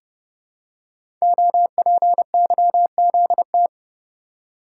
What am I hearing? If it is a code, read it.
Morse code, heard as a keyed tone, about 30 wpm, 700 Hz. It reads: OPYZT
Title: Code Groups - 5 Letters - 30wpm